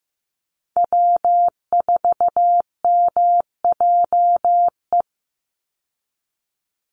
Morse code, heard as a keyed tone, about 15 wpm, 700 Hz. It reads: W4MJE
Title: Call Signs Repeated 3x - 15wpm